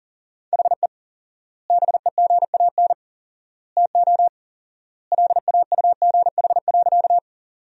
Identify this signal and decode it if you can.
Morse code, heard as a keyed tone, about 40 wpm, 700 Hz. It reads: HE BEGAN TO LAUGH.
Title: 4x Word-Spacing - Sentences from Top 500 Words - 40wpm